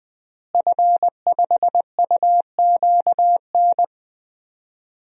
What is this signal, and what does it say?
Morse code, heard as a keyed tone, about 20 wpm, 700 Hz. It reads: F5UQN
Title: DXCC-100 Call Signs - 5 Characters - Repeated 3x - 20wpm